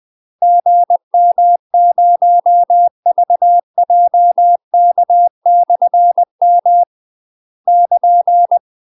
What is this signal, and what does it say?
Morse code, heard as a keyed tone, about 20 wpm, 700 Hz. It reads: GM0VJK/M <KN>
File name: UK MASTER QSOs TRAINING/20WPM